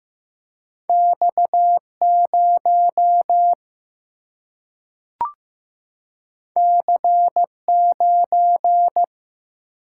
Morse code, heard as a keyed tone, about 15 wpm, 700 Hz. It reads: X0 C9
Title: Single-Letter Single-Number 15wpm